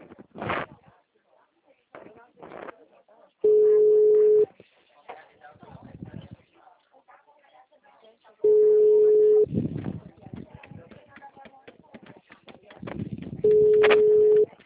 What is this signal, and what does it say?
Ringback tone (Europe, 425 Hz), with speech.